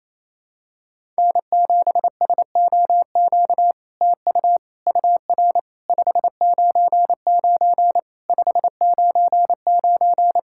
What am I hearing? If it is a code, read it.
Morse code, heard as a keyed tone, about 28 wpm, 700 Hz. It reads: N7SOQ TU UR 599 599
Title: SOTA QSO Practice 28wpm